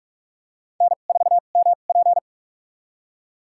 Morse code, heard as a keyed tone, about 42 wpm, spaced as for 31 wpm, 700 Hz. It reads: N4MP
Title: USA Call Signs - 4 Characters - 2x Character Spacing - 45wpm